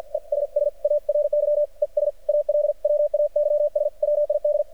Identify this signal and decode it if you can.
Morse code, heard as a keyed tone, about 48 wpm, 590 Hz. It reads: ESSARY ES ALWAYS WID
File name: archived / CRSnet